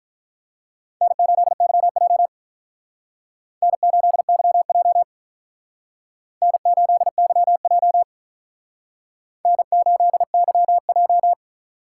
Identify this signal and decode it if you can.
Morse code, heard as a keyed tone, about 44 wpm, 700 Hz. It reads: N8YJ N8YJ N8YJ N8YJ